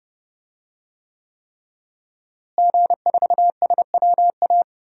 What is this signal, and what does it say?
Morse code, heard as a keyed tone, about 30 wpm, 700 Hz. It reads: G4SWA
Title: DXCC-100 Call Signs - 5 Characters - Repeated 3x - 30wpm